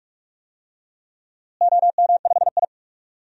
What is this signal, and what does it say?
Morse code, heard as a keyed tone, about 46 wpm, 700 Hz. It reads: OM5I
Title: DXCC-100 Call Signs - 4 Characters - Repeated 3x - 45wpm